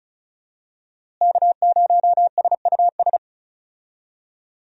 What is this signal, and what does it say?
Morse code, heard as a keyed tone, about 34 wpm, 700 Hz. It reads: K0SUS
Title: USA Call Signs - 5 Characters - Repeated 2x - 35wpm